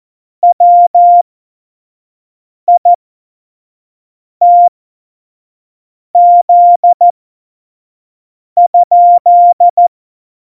Morse code, heard as a keyed tone, about 14 wpm, spaced as for 5 wpm, 700 Hz. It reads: WITZ?
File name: Morsekurs / Witze / Witz 17. November 2020, 25 F70